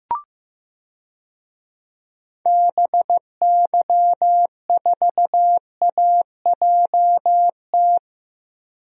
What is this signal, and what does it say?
Morse code, heard as a keyed tone, about 15 wpm, 700 Hz. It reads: BY4AJT
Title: DXCC-100 Call Signs - 6 Characters - Repeated 2x - 15wpm